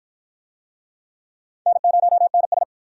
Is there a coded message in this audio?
Yes: Morse code from 1.7 to 2.6 s, heard as a keyed tone, about 48 wpm, 700 Hz. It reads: N0NS